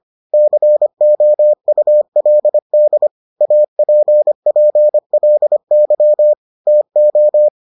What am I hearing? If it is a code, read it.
Morse code, heard as a keyed tone, about 25 wpm, 600 Hz. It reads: COULD APPLY TO